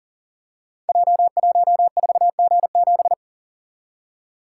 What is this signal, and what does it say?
Morse code, heard as a keyed tone, about 40 wpm, 700 Hz. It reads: J14G7
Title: Code Groups - 5 Letter:Numbers - 40wpm